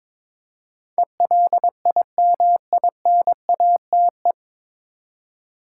Morse code, heard as a keyed tone, about 21 wpm, 700 Hz. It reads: ELIMINATE
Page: Morse Code Ninja practice 22wpm